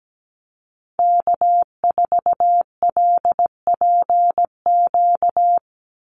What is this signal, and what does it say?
Morse code, heard as a keyed tone, about 17 wpm, 700 Hz. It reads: K4LPQ
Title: NA SOTA Activator Callsigns 17wpm